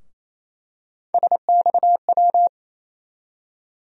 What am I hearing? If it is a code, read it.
Morse code, heard as a keyed tone, about 28 wpm, 700 Hz. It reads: SXW
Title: USA Call Sign Postfixes - 3 Letters 28wpm